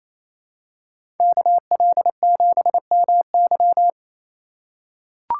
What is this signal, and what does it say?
Morse code, heard as a keyed tone, about 27 wpm, 700 Hz. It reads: KL7MY
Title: USA Call Signs - 5 Characters - 4x Character Spacing - 28wpm